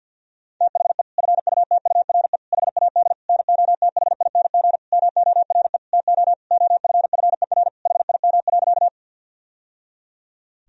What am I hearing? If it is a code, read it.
Morse code, heard as a keyed tone, about 52 wpm, 700 Hz. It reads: THE FUTURE HAD NOTHING MORE TO OFFER HIM.
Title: Sentences from Top 1000 Words - Review Part 4 - 50wpm